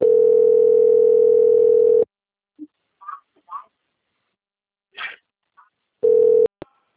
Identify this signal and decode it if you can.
Ringback tone (North America), 440 + 480 Hz.